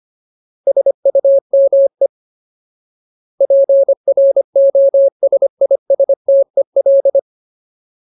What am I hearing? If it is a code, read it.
Morse code, heard as a keyed tone, about 25 wpm, 550 Hz. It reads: SUME PROSISTEL
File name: Ultimate Word List / 25WPM Ultimate